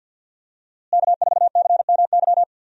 Morse code, heard as a keyed tone, about 50 wpm, 700 Hz. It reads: K4CMY
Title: USA Call Signs - 5 Characters - Repeated 2x - 50wpm